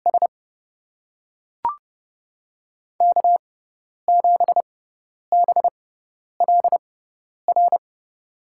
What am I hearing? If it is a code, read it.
Morse code, heard as a keyed tone, about 30 wpm, spaced as for 10 wpm, 700 Hz. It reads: S K7BLR
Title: USA Call Signs - 5 Characters - 6x Character Spacing - 30wpm